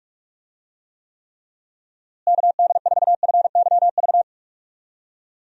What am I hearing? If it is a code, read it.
Morse code, heard as a keyed tone, about 46 wpm, 700 Hz. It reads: KD4FYV
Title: USA Call Signs - 6 Characters - Repeated 3x - 45wpm